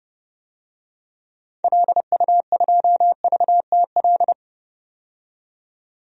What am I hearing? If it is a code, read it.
Morse code, heard as a keyed tone, about 30 wpm, 700 Hz. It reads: LU2VTL
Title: DXCC-100 Call Signs - 6 Characters - 30wpm